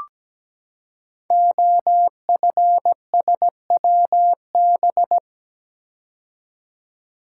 Morse code, heard as a keyed tone, about 17 wpm, 700 Hz. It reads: OFSWB
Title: Code Groups - 5 Letters - 17wpm